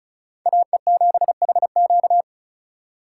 Morse code, heard as a keyed tone, about 39 wpm, 700 Hz. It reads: AE7HQ
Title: Speed Racing - USA 2x2 Call Signs 35wpm